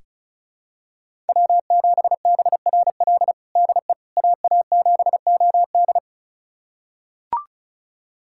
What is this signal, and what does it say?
Morse code, heard as a keyed tone, about 36 wpm, 700 Hz. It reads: W7BRL DE AA7OD
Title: USA Callsign DE Callsign - 5 Characters - 35wpm